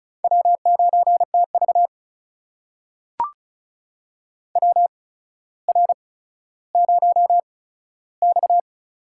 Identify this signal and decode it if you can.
Morse code, heard as a keyed tone, about 36 wpm, spaced as for 15 wpm, 700 Hz. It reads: W9TV WR0X
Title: USA Call Signs - 4 Characters - 8x Character Spacing - 35wpm